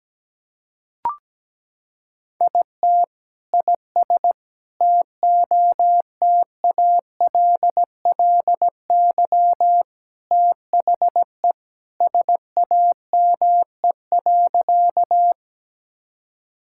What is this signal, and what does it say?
Morse code, heard as a keyed tone, about 17 wpm, 700 Hz. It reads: IT IS TOTALLY THE SAME.
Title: Sentences from Top 1600 Words 17wpm